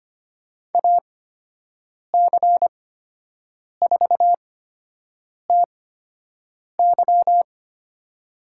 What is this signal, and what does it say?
Morse code, heard as a keyed tone, about 25 wpm, spaced as for 7 wpm, 700 Hz. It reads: AC4TY
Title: USA Call Signs - 5 Characters - 8x Character Spacing - 25wpm